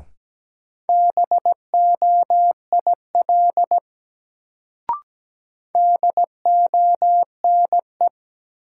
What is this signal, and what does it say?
Morse code, heard as a keyed tone, about 17 wpm, 700 Hz. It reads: BOIL DONE
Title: Sets of 5 Words Encore - New Word Drill - 17wpm